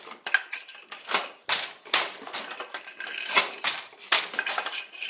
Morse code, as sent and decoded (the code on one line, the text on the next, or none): none